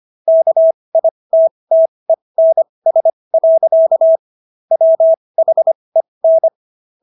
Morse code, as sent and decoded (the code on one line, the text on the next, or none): -.- .. - - . -. ... .-.-.- / .-- .... -.
KITTENS. WHEN